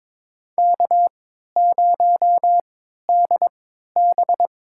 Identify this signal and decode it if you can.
Morse code, heard as a keyed tone, about 22 wpm, spaced as for 12 wpm, 700 Hz. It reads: K0DB
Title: USA Call Signs - 4 Characters - 3x Character Spacing - 22wpm